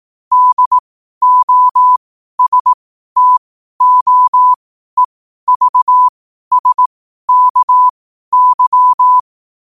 Morse code, heard as a keyed tone, about 18 wpm, spaced as for 12 wpm, 1000 Hz. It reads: DOSTOEVSKY
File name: qotdfiles / 12wpm